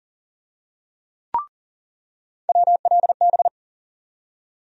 Morse code, heard as a keyed tone, about 39 wpm, 700 Hz. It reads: WLB
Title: USA Call Sign Postfixes - 3 Letters 40wpm